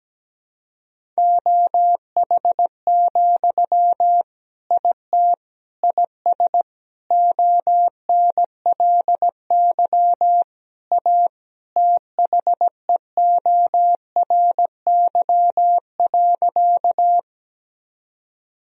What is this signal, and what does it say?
Morse code, heard as a keyed tone, about 17 wpm, 700 Hz. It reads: OH, IT IS ONLY A THEORY.
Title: Sentences from Top 1000 Words - Review Part 3 - 17wpm